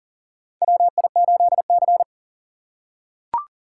Morse code, heard as a keyed tone, about 37 wpm, 690 Hz. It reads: WI8C
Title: USA Call Signs - 4 Characters - 6x Character Spacing - 40wpm